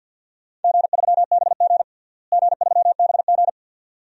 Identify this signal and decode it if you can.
Morse code, heard as a keyed tone, about 52 wpm, 700 Hz. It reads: G3BG G3BG
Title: DXCC-100 Call Signs - 4 Characters - Repeated 2x - 50wpm